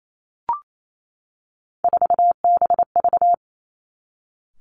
Morse code, heard as a keyed tone, about 28 wpm, 700 Hz. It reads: 4BV
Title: Three Challenging Letter-Number Combinations 28wpm